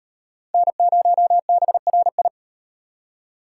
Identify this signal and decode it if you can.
Morse code, heard as a keyed tone, about 35 wpm, 700 Hz. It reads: N0BRI